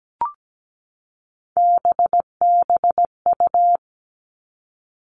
Morse code, heard as a keyed tone, about 17 wpm, 700 Hz. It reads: BBU
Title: Three Challenging Letter-Number Combinations 17wpm